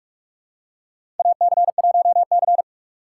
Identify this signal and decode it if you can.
Morse code, heard as a keyed tone, about 46 wpm, 700 Hz. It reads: AC1C
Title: USA Call Signs - 4 Characters - 3x Character Spacing - 45wpm